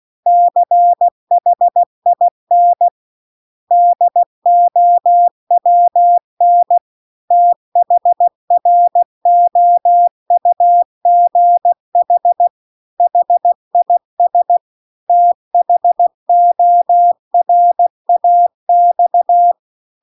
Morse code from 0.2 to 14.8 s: CHIN DOWN THROUGH HIS